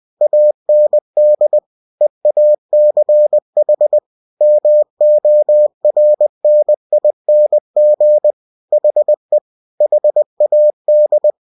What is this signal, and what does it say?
Morse code, heard as a keyed tone, about 20 wpm, 600 Hz. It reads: AND EACH MORNING HE HAD
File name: cw-books / winnie-the-pooh / winnie-20wpm-page-0096